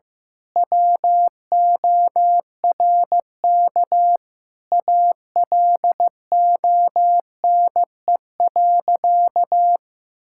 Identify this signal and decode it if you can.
Morse code, heard as a keyed tone, about 15 wpm, 700 Hz. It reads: WORK ALONE.
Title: Sentences from Top 1600 Words 15wpm